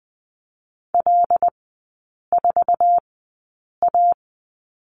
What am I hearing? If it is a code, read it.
Morse code, heard as a keyed tone, about 20 wpm, spaced as for 8 wpm, 700 Hz. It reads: L4A
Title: 081 Lesson 14e - Number 2 - Three Characters 20wpm